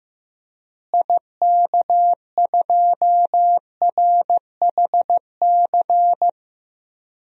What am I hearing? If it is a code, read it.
Morse code, heard as a keyed tone, about 15 wpm, 700 Hz. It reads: IK2RHC
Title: DXCC-100 Call Signs - 6 Characters - Repeated 2x - 15wpm